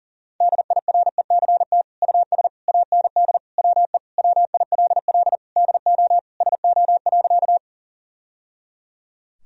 Morse code, heard as a keyed tone, about 40 wpm, 700 Hz. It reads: DIRECT US AND WE WILL DO SO.
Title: Sentences from Top 1000 Words - Review Part 2 - 40wpm